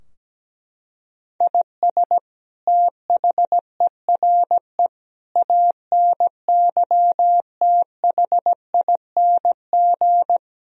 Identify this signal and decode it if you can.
Morse code, heard as a keyed tone, about 17 wpm, 700 Hz. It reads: IS THERE ANYTHING
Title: Sentences from Top 1000 Words - Review Part 4 - 17wpm